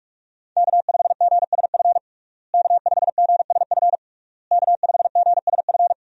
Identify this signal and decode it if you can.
Morse code, heard as a keyed tone, about 46 wpm, 700 Hz. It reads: K5GSF K5GSF K5GSF
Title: USA Call Signs - 5 Characters - Repeated 3x - 45wpm